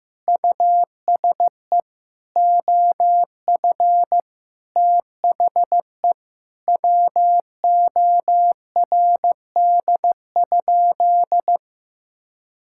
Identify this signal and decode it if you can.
Morse code, heard as a keyed tone, about 15 wpm, 700 Hz. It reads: USE OF THE WORD?
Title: Sentences from Top 1000 Words - Review Part 6 - 15wpm